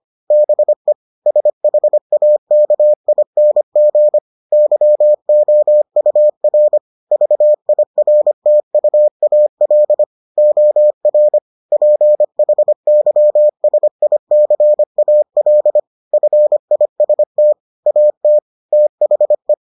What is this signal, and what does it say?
Morse code, heard as a keyed tone, about 25 wpm, 600 Hz. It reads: BE SHAKING YOUR VIRTUAL OR PHYSICAL FIST AT THE